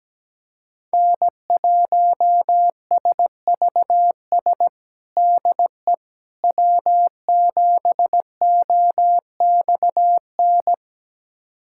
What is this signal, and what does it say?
Morse code, heard as a keyed tone, about 17 wpm, 700 Hz. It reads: N1SVS DE W7OXN
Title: USA Callsign DE Callsign - 5 Characters - 17wpm